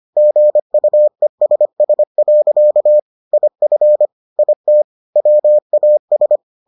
Morse code, heard as a keyed tone, about 25 wpm, 600 Hz. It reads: GUESS. IF IT WAS